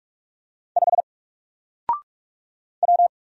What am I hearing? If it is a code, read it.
Morse code, heard as a keyed tone, about 44 wpm, 700 Hz. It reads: 5 W
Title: Single Character (Letter:Number:Punctuation:Prosign) Alt 45wpm